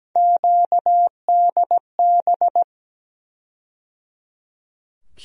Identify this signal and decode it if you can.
Morse code, heard as a keyed tone, about 17 wpm, 700 Hz. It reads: QDB
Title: USA Call Sign Postfixes - 3 Letters 17wpm